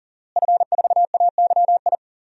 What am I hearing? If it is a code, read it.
Morse code, heard as a keyed tone, about 40 wpm, 700 Hz. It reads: F4AYI
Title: DXCC-100 Call Signs - 5 Characters - Repeated 2x - 40wpm